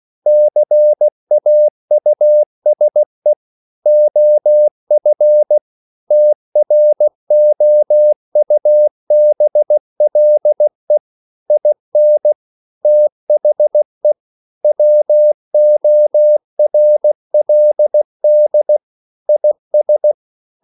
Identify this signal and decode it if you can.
Morse code, heard as a keyed tone, about 16 wpm, 600 Hz. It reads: CAUSE OF TROUBLE IN THE WORLD IS
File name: CW / Practice / EnglishQuotations0116